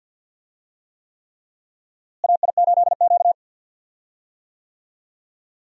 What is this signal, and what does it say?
Morse code, heard as a keyed tone, about 53 wpm, 700 Hz. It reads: AI8Q
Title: USA Call Signs - 4 Characters - Repeated 3x - 50wpm